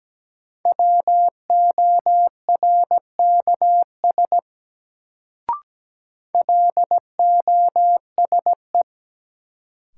Morse code, heard as a keyed tone, about 17 wpm, 700 Hz. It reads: WORKS LOSE
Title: Sentences from Top 1100 Words - New Word Drill Alt - 17wpm